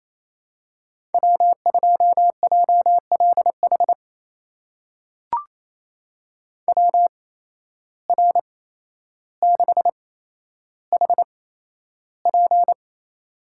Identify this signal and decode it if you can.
Morse code, heard as a keyed tone, about 28 wpm, spaced as for 12 wpm, 700 Hz. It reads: W2JLH WR6HP
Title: USA Call Signs - 5 Characters - 8x Character Spacing - 28wpm